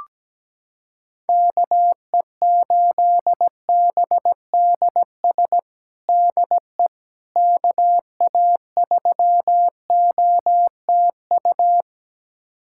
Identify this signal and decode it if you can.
Morse code, heard as a keyed tone, about 17 wpm, 700 Hz. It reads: KE8BDS DE KA3OTU
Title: USA Callsign DE Callsign 17wpm